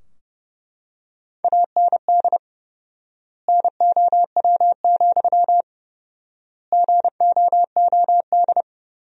Morse code, heard as a keyed tone, about 30 wpm, 700 Hz. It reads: AND NOW, GOOD
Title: 4x Word-Spacing - Sentences from Top 300 Words - 30wpm